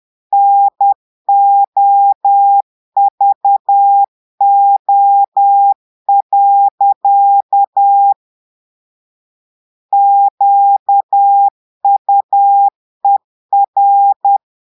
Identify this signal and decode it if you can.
Morse code, heard as a keyed tone, about 10 wpm, 800 Hz. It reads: NOVO. QUER
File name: portugais texte en clair 2 vitesse 10 mots minutes